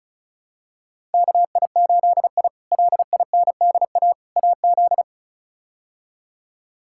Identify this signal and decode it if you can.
Morse code, heard as a keyed tone, about 36 wpm, 700 Hz. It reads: KI8I LINDA AZ
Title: K1USN SST QSO Practice - 35wpm